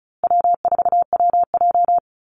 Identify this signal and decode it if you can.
Morse code, heard as a keyed tone, about 34 wpm, 700 Hz. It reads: W4WJ
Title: Call Signs Repeated 2x - 35wpm